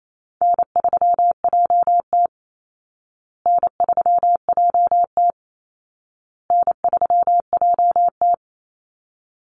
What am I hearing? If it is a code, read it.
Morse code, heard as a keyed tone, about 28 wpm, 700 Hz. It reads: N3JT N3JT N3JT
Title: Easy Call Signs Repeated 3x - 28wpm